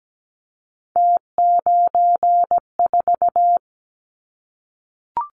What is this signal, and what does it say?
Morse code, heard as a keyed tone, about 17 wpm, 700 Hz. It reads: T94
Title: Callsign Prefixes for Top 50 DXCC - 17wpm